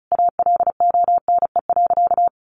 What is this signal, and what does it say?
Morse code, heard as a keyed tone, about 34 wpm, 700 Hz. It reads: ALONE.
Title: Sentences from Top 1000 Words 35wpm